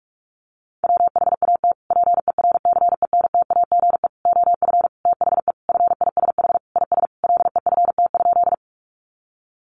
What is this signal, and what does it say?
Morse code, heard as a keyed tone, about 44 wpm, 700 Hz. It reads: WHAT PERCENTAGE OF THE FISH IS LEFT?
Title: Sentences from Top 1300 Words 45wpm